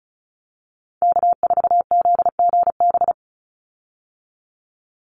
Morse code, heard as a keyed tone, about 34 wpm, 700 Hz. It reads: K4ZGB
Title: Call Signs Repeated 2x - 35wpm